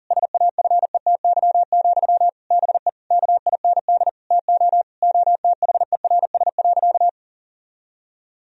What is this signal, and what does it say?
Morse code, heard as a keyed tone, about 40 wpm, 700 Hz. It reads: SAFETY, BE KIND TO OTHERS.
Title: Sentences from Top 1000 Words - Review Part 4 - 40wpm